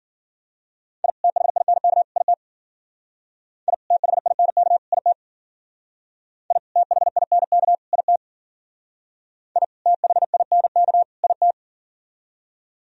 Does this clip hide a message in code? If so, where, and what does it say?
Yes: Morse code from 1.0 to 11.5 s, heard as a keyed tone, about 50 wpm, 700 Hz. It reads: I THINK IT I THINK IT I THINK IT I THINK IT